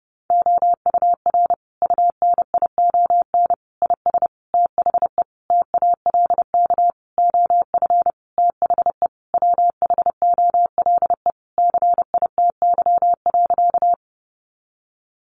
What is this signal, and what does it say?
Morse code, heard as a keyed tone, about 30 wpm, 700 Hz. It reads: OUR UNION IS THE TALK OF THE WHOLE CITY.
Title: Sentences from Top 1400 Words 30wpm